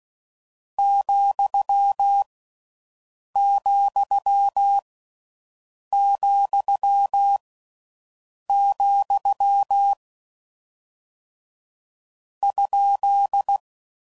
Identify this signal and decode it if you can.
Morse code, heard as a keyed tone, about 16 wpm, 1350 Hz. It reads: , , , , ?